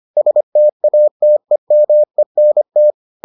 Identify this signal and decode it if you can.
Morse code, heard as a keyed tone, about 25 wpm, 600 Hz. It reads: STATEMENT